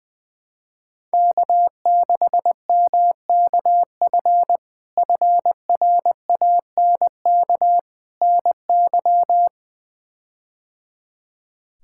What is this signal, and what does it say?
Morse code, heard as a keyed tone, about 21 wpm, 700 Hz. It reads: K6MKF FRANK NY
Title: K1USN SST QSO Practice - 20wpm